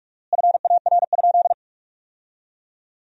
Morse code, heard as a keyed tone, about 47 wpm, 700 Hz. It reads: FAR?
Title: 4x Word-Spacing - Sentences from Top 300 Words - 45wpm